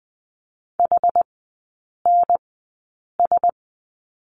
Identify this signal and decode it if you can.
Morse code, heard as a keyed tone, about 20 wpm, spaced as for 8 wpm, 700 Hz. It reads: HNS
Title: 074 Lesson 13e - Letter L - Three Characters 20wpm